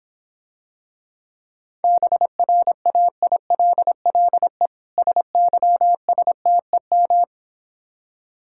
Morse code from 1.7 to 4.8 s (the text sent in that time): BRAILLE